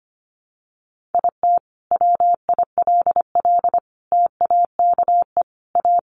Morse code, heard as a keyed tone, about 25 wpm, 700 Hz. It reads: IT WILL TAKE A
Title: Sentences from Top 1300 Words 25wpm